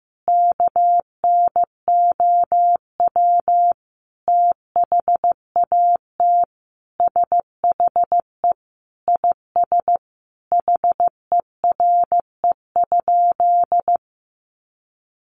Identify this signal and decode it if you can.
Morse code, heard as a keyed tone, about 15 wpm, 700 Hz. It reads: KNOW THAT SHE IS HERE?
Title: Sentences from Top 1300 Words 15wpm